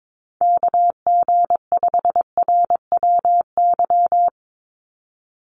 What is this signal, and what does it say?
Morse code, heard as a keyed tone, about 22 wpm, 700 Hz. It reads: KG5RWY
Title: USA Call Signs 22wpm